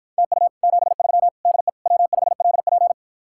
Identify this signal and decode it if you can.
Morse code, heard as a keyed tone, about 54 wpm, 700 Hz. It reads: TU 73 DE W5LP